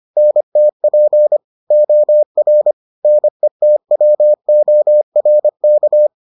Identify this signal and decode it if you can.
Morse code, heard as a keyed tone, about 25 wpm, 600 Hz. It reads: NTP OR NETWORK